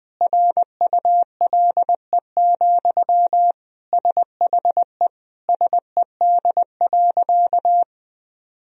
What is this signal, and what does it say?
Morse code, heard as a keyed tone, about 20 wpm, 700 Hz. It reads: RULE, SHE SED.
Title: Sentences from Top 400 Words and Abbreviations 20wpm